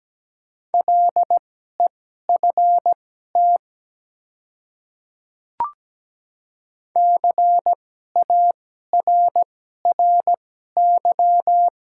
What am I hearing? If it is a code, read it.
Morse code, heard as a keyed tone, about 17 wpm, spaced as for 12 wpm, 700 Hz. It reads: LEFT CARRY
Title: 2x Character Spacing - Top 300 Words - 17wpm